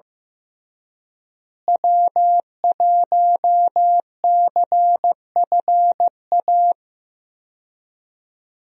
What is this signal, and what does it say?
Morse code, heard as a keyed tone, about 16 wpm, 700 Hz. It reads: W1CFA
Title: USA Call Signs - 5 Characters - Repeated 3x - 15wpm